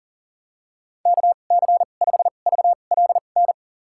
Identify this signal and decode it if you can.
Morse code, heard as a keyed tone, about 40 wpm, spaced as for 29 wpm, 700 Hz. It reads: KC5VLN
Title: USA Call Signs - 2x Character Spacing - 40wpm